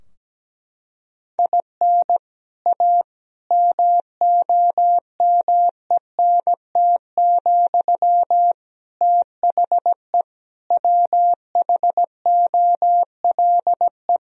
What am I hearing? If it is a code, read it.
Morse code, heard as a keyed tone, about 17 wpm, 700 Hz. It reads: IN A MOMENT, THE WHOLE
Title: Sentences from Top 1000 Words - Review Part 3 - 17wpm